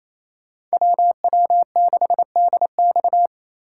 Morse code, heard as a keyed tone, about 29 wpm, 700 Hz. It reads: WW6DX
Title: Speed Racing - USA 2x2 Call Signs 28wpm